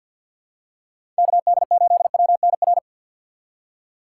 Morse code, heard as a keyed tone, about 51 wpm, 700 Hz. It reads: KD8WNR
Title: USA Call Signs - 6 Characters - 4x Character Spacing - 50wpm